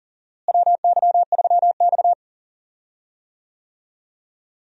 Morse code, heard as a keyed tone, about 37 wpm, 700 Hz. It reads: WY3X
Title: USA Call Signs - 4 Characters - Repeated 2x - 40wpm